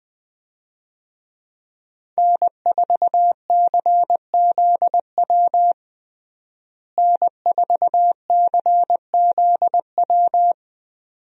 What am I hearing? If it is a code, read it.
Morse code, heard as a keyed tone, about 20 wpm, 700 Hz. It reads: N4CZW N4CZW
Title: USA Call Signs - 5 Characters - Repeated 2x - 20wpm